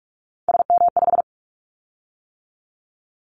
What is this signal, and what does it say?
Morse code, heard as a keyed tone, about 46 wpm, 700 Hz. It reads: SM5
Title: Callsign Prefixes for Top 25 DXCC - 45wpm